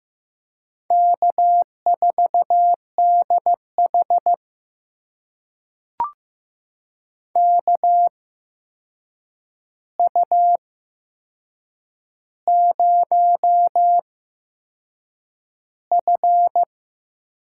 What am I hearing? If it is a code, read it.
Morse code, heard as a keyed tone, about 16 wpm, spaced as for 7 wpm, 700 Hz. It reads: K4DH KU0F
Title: USA Call Signs - 4 Characters - 8x Character Spacing - 15wpm